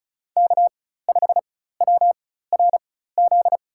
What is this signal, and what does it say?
Morse code, heard as a keyed tone, about 36 wpm, spaced as for 16 wpm, 700 Hz. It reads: K5WRZ
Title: USA Call Signs - 5 Characters - 4x Character Spacing - 35wpm